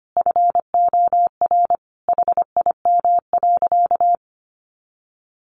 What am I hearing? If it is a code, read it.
Morse code, heard as a keyed tone, about 25 wpm, 700 Hz. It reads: FOR HIM.
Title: Sentences from Top 1000 Words 25wpm